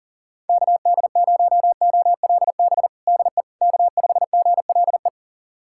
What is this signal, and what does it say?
Morse code, heard as a keyed tone, about 40 wpm, 700 Hz. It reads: KD0OLB DE K5GLE
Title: USA Callsign DE Callsign 40wpm